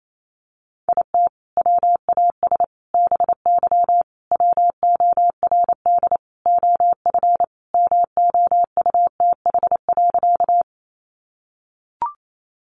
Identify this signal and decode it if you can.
Morse code, heard as a keyed tone, about 28 wpm, 700 Hz. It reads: IT WAS BY WORD OF MOUTH.
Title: Sentences from Top 1500 Words 28wpm